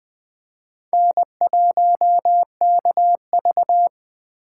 Morse code, heard as a keyed tone, about 24 wpm, 700 Hz. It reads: N1KV
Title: Speed Racing - USA 1x2 Call Signs 20wpm